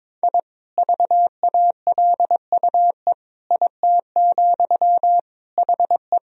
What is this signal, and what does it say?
Morse code, heard as a keyed tone, about 22 wpm, 700 Hz. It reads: I VALUE IT, HE
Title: Sentences from Top 1000 Words - Review Part 3 - 22wpm